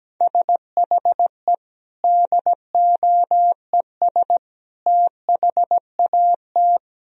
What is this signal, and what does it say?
Morse code, heard as a keyed tone, about 17 wpm, 700 Hz. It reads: SHE DOES THAT
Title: Sentences from Top 1000 Words - Review Part 1 - 17wpm